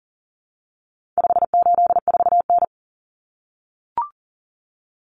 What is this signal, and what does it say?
Morse code, heard as a keyed tone, about 40 wpm, 700 Hz. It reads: 584N
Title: Common Cut Numbers - 4-Digit 40wpm